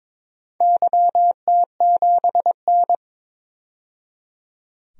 Morse code, heard as a keyed tone, about 23 wpm, 700 Hz. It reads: YT7N
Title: DXCC-100 Call Signs - 4 Characters - 22wpm